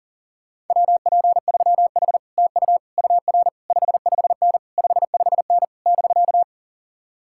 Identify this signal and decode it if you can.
Morse code, heard as a keyed tone, about 49 wpm, 700 Hz. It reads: WP3H TU UR 55N 55N <BK>
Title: Speed-Racing Common QSO Phrases - 40wpm